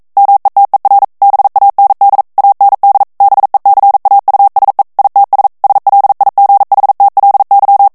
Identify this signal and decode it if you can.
Morse code, heard as a keyed tone, about 42 wpm, 800 Hz. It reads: METER BAND AND BECAUSE ITS SLIGHTLY